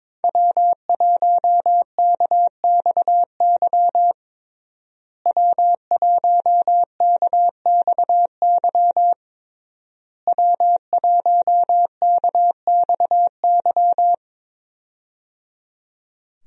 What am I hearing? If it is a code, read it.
Morse code, heard as a keyed tone, about 21 wpm, 700 Hz. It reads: W1KXY W1KXY W1KXY
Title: USA Call Signs - 5 Characters - Repeated 3x - 22wpm